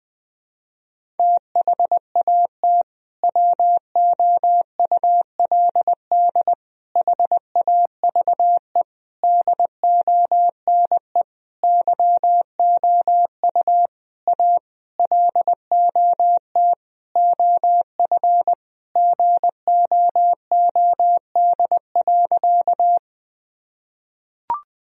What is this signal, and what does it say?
Morse code, heard as a keyed tone, about 20 wpm, 700 Hz. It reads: THAT WOULD HAVE DONE YOU A LOT OF GOOD.
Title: Sentences from Top 1000 Words - Review Part 3 - 20wpm